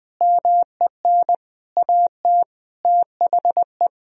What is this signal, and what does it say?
Morse code, heard as a keyed tone, about 21 wpm, 700 Hz. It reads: MEN AT THE
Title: Sentences from Top 1000 Words - Review Part 4 - 20wpm